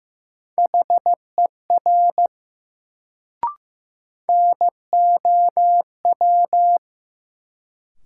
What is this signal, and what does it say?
Morse code, heard as a keyed tone, about 15 wpm, 700 Hz. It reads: HER NOW
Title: Sentences from Top 100 Words - New Word Drill - 15wpm